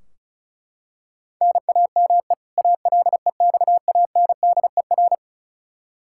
Morse code, heard as a keyed tone, about 34 wpm, 700 Hz. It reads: NAME ALEXANDER
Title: Practice QSO - Exchange 1 - Beginning - 35wpm